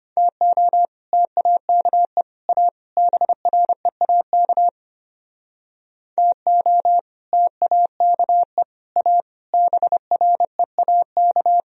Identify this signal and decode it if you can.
Morse code, heard as a keyed tone, about 28 wpm, 700 Hz. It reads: TO TAKE A BREAK TO TAKE A BREAK
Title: Speed-Racing - Sets of 4 Words 25wpm